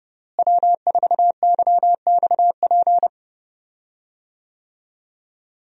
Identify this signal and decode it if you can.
Morse code, heard as a keyed tone, about 31 wpm, 700 Hz. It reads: W4YXP
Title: USA Call Signs - 5 Characters - Repeated 3x - 30wpm